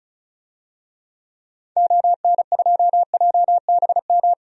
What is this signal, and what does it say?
Morse code, heard as a keyed tone, about 36 wpm, 700 Hz. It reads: ON2JBM
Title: DXCC-100 Call Signs - 6 Characters - 35wpm